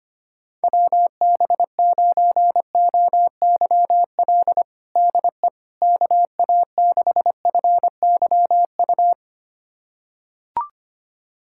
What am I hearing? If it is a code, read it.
Morse code, heard as a keyed tone, about 25 wpm, 700 Hz. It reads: WB9OYL DE KA6FYU
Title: USA Callsign DE Callsign - 6 Characters - 25wpm